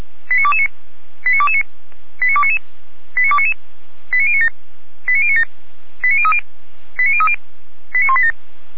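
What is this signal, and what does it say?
Five-tone selective call (ZVEI), heard as quick runs of five tones, tones of about 70 ms. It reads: 89229 89229 89299 89299 89097 89097 89030 89030 89187
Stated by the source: ZVEI five-tone call received on SDR, Feuerwehr Orlamünde.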